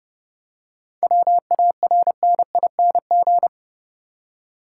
Morse code, heard as a keyed tone, about 30 wpm, 700 Hz. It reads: WARNING
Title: Seven Letter Words - 30wpm